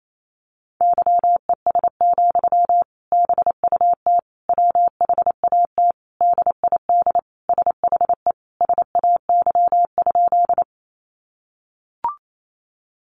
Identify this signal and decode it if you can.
Morse code, heard as a keyed tone, about 28 wpm, 700 Hz. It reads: YES, BUT WHAT DID SHE SAY?
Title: Sentences from Top 500 Words 28wpm